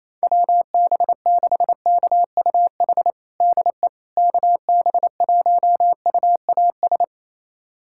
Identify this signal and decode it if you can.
Morse code, heard as a keyed tone, about 28 wpm, 700 Hz. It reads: WB6KUH DE KB1UAS
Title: USA Callsign DE Callsign - 6 Characters - 28wpm